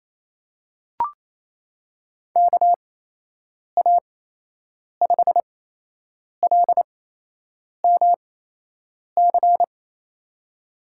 Morse code, heard as a keyed tone, about 28 wpm, spaced as for 8 wpm, 700 Hz. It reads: KA5LMC
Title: USA Call Signs - 6 Characters - 8x Character Spacing - 28wpm